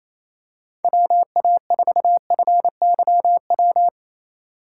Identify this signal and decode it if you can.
Morse code, heard as a keyed tone, about 28 wpm, 700 Hz. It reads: WA4FYW
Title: USA Call Signs - 6 Characters - 4x Character Spacing - 28wpm